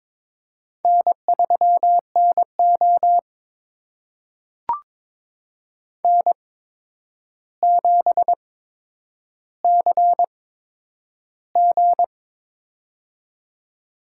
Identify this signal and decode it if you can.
Morse code, heard as a keyed tone, about 22 wpm, spaced as for 9 wpm, 700 Hz. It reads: N3NO N7CG
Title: USA Call Signs - 4 Characters - 8x Character Spacing - 22wpm